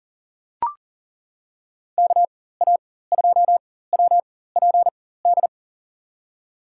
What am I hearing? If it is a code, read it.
Morse code, heard as a keyed tone, about 39 wpm, spaced as for 19 wpm, 700 Hz. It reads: KA2WPD
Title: USA Call Signs - 6 Characters - 4x Character Spacing - 40wpm